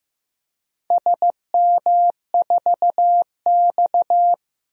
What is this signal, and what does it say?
Morse code, heard as a keyed tone, about 15 wpm, 700 Hz. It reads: SM4X
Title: DXCC-100 Call Signs - 4 Characters - Repeated 3x - 15wpm